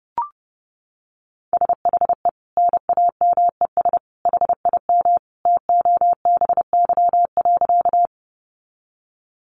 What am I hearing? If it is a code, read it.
Morse code, heard as a keyed tone, about 30 wpm, 700 Hz. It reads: SHE NAMES HIM TOBY.
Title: Easy Phrases 30wpm